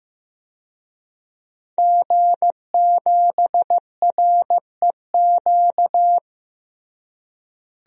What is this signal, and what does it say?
Morse code, heard as a keyed tone, about 16 wpm, 700 Hz. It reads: G7REQ
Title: DXCC-100 Call Signs - 5 Characters - Repeated 3x - 15wpm